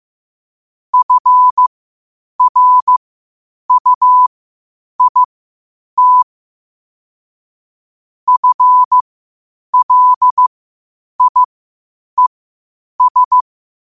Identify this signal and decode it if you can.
Morse code, heard as a keyed tone, about 15 wpm, spaced as for 8 wpm, 1000 Hz. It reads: FRUIT FLIES